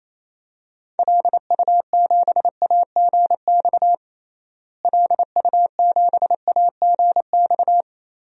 Morse code, heard as a keyed tone, about 28 wpm, 700 Hz. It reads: LU7AGX LU7AGX
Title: DXCC-100 Call Signs - 6 Characters - Repeated 3x - 28wpm